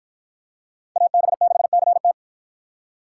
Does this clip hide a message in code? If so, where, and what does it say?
Yes: Morse code from 1.0 to 2.1 s, heard as a keyed tone, about 52 wpm, 700 Hz. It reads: AB6CT